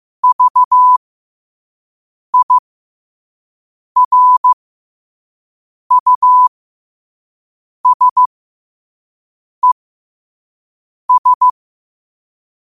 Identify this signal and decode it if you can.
Morse code, heard as a keyed tone, about 15 wpm, spaced as for 5 wpm, 1000 Hz. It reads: VIRUSES
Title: 5 WPM morse code quotes for Fri, 01 Mar 2024